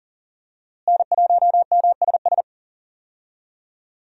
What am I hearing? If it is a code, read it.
Morse code, heard as a keyed tone, about 37 wpm, 700 Hz. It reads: N1MSS
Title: USA Call Signs - 5 Characters - Repeated 2x - 40wpm